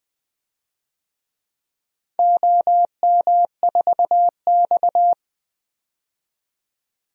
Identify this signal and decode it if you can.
Morse code, heard as a keyed tone, about 21 wpm, 700 Hz. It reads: OM4X
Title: DXCC-100 Call Signs - 4 Characters - 20wpm